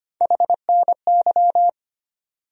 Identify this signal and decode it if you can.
Morse code, heard as a keyed tone, about 25 wpm, 700 Hz. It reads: HNY
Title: USA Call Sign Postfixes - 3 Letters 25wpm